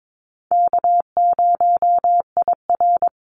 Morse code, heard as a keyed tone, about 22 wpm, 700 Hz. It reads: K0IR
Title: Easy Call Signs - 22wpm